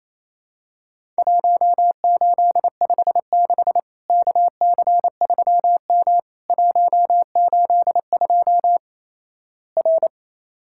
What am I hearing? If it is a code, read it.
Morse code, heard as a keyed tone, about 28 wpm, 700 Hz. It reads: 1856 KC3M 182 R
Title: Worked All Europe DX Contest - QTC Exchange - 28wpm